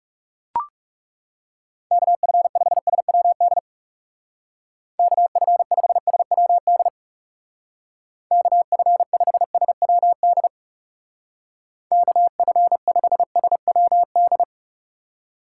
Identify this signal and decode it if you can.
Morse code, heard as a keyed tone, about 38 wpm, 700 Hz. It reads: KF5SWD KF5SWD KF5SWD KF5SWD